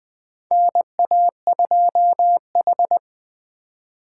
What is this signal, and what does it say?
Morse code, heard as a keyed tone, about 20 wpm, 700 Hz. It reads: NA2H
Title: USA Call Signs - 4 Characters - 4x Character Spacing - 20wpm